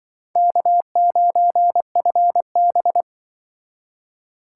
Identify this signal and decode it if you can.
Morse code, heard as a keyed tone, about 24 wpm, 700 Hz. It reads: K9FB
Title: Speed Racing - USA 1x2 Call Signs 20wpm